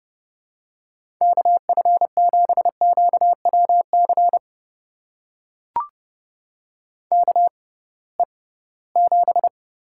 Morse code, heard as a keyed tone, about 31 wpm, spaced as for 20 wpm, 700 Hz. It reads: KF7QWC KE7
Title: USA Call Signs - 6 Characters - 6x Character Spacing - 30wpm